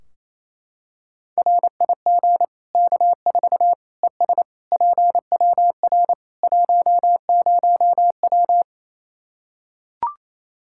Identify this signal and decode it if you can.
Morse code, heard as a keyed tone, about 28 wpm, 700 Hz. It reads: RIG K4 ES PWR 10W
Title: Practice QSO - Exchange 2 - Info - 28wpm